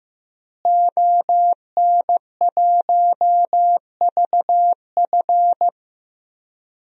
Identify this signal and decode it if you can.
Morse code, heard as a keyed tone, about 15 wpm, 700 Hz. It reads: ON1VF
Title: DXCC-100 Call Signs - 5 Characters - Repeated 3x - 15wpm